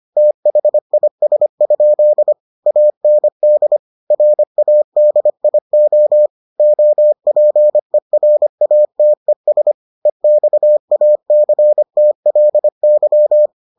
Morse code, heard as a keyed tone, about 25 wpm, 600 Hz. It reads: THIS? AND RADIO OPERATES EXACTLY